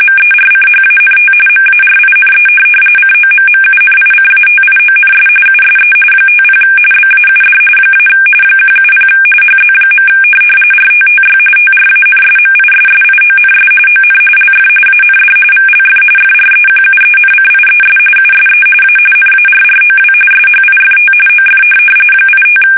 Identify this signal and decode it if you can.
RTTY, about 45.45 baud. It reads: MERCHANT MARINE COMMUNICATIONS USE 50 BAUD - 170 HZ. RTTY TRANSMISSIONS ON LF AND VLF FREQUENCIES USE A NARROW SHIFT OF 85
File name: fsk850 / teletype-RTTY